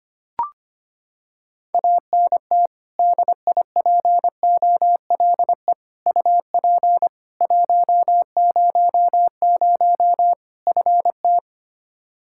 Morse code, heard as a keyed tone, about 25 wpm, 700 Hz. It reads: ANT DIPOLE UP 100 FT
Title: Practice QSO - Exchange 2 - Info - 25wpm